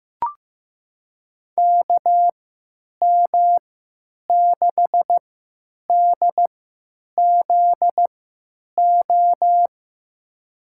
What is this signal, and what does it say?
Morse code, heard as a keyed tone, about 15 wpm, spaced as for 9 wpm, 700 Hz. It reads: KM6DZO
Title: USA Call Signs - 3x Character Spacing - 15wpm